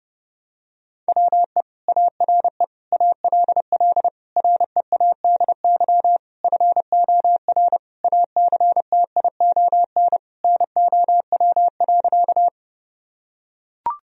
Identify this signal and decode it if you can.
Morse code, heard as a keyed tone, about 30 wpm, 700 Hz. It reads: WE ARE ALL READY FOR ACTION NOW.
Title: Sentences from Top 1000 Words - Review Part 4 - 30wpm